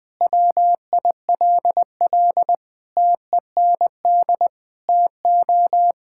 Morse code, heard as a keyed tone, about 21 wpm, 700 Hz. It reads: WILL TEND TO
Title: Sentences from Top 1000 Words - Review Part 5 - 20wpm